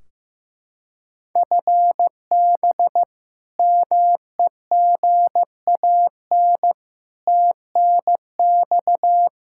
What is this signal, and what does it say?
Morse code, heard as a keyed tone, about 15 wpm, 700 Hz. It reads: FB MEGAN TNX
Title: Practice QSO - Exchange 3 - More Info - 15wpm